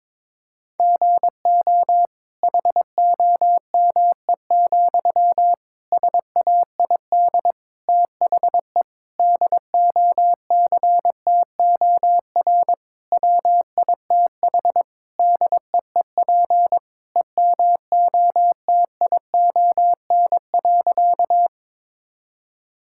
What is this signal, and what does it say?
Morse code, heard as a keyed tone, about 22 wpm, 700 Hz. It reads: GO HOME, SAID THE DOCTOR WITH DEEP EMOTION.
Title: Sentences from Top 1600 Words 22wpm